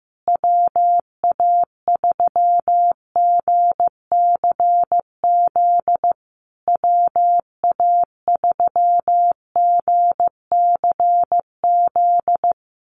Morse code, heard as a keyed tone, about 15 wpm, 700 Hz. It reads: WA3GCZ WA3GCZ
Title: USA Call Signs Repeated 3x 15wpm